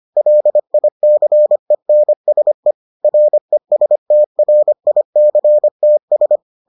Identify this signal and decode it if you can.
Morse code, heard as a keyed tone, about 25 wpm, 600 Hz. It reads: LICENSE RESTRICTS